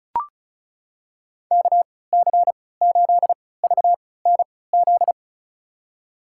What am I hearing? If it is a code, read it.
Morse code, heard as a keyed tone, about 34 wpm, spaced as for 20 wpm, 690 Hz. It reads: KC8VNZ
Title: USA Call Signs - 3x Character Spacing - 35wpm